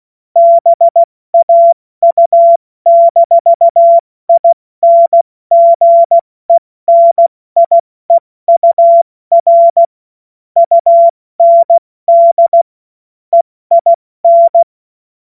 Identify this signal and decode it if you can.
Morse code, heard as a keyed tone, about 16 wpm, 680 Hz. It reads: BAU-INGENIEUR UND EIN
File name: Morsekurs / Witze / Witz 21.November 2023 70 F80